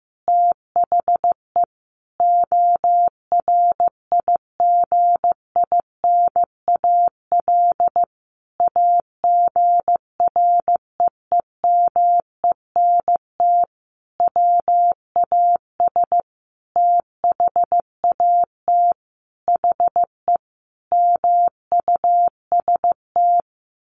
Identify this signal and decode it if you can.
Morse code, heard as a keyed tone, about 15 wpm, 700 Hz. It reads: THE ORIGINAL AGREEMENT WAS THAT HE MUST
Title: Sentences from Top 1500 Words 15wpm